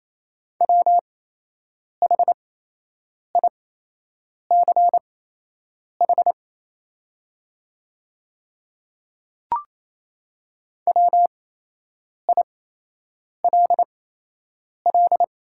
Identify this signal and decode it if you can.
Morse code, heard as a keyed tone, about 28 wpm, spaced as for 8 wpm, 690 Hz. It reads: WHICH WILL
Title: 8x Character Spacing - Top 100 Words - 28wpm